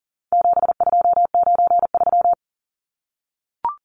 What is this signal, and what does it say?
Morse code, heard as a keyed tone, about 40 wpm, 700 Hz. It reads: 7293
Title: Numbers - 4-Digit 40wpm